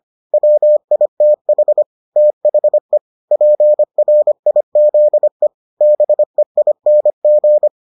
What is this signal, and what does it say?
Morse code, heard as a keyed tone, about 25 wpm, 600 Hz. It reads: WITH THE PRIZE BEING